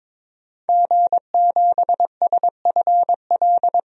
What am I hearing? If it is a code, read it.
Morse code, heard as a keyed tone, about 22 wpm, 700 Hz. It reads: G7SFL
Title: DXCC-100 Call Signs - 5 Characters - 22wpm